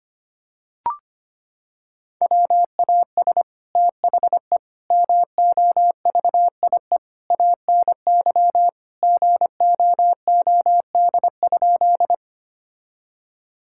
Morse code, heard as a keyed tone, about 25 wpm, 700 Hz. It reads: WAS THE MOVIE ANY GOOD?
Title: Sentences from Top 1000 Words - Review Part 5 - 25wpm